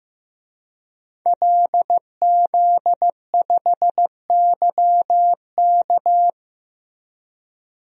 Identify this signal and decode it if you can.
Morse code, heard as a keyed tone, about 16 wpm, 700 Hz. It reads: LZ5YK
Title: DXCC-100 Call Signs - 5 Characters - Repeated 2x - 15wpm